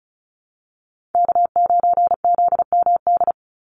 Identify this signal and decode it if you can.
Morse code, heard as a keyed tone, about 34 wpm, 700 Hz. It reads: K9ZMD
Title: NA SOTA Chaser Callsigns 35wpm